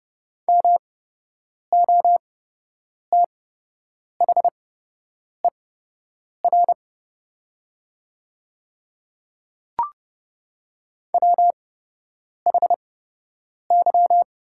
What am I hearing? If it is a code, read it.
Morse code, heard as a keyed tone, about 31 wpm, spaced as for 8 wpm, 700 Hz. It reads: MOTHER WHY
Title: 8x Character Spacing - Top 200 Words - 30wpm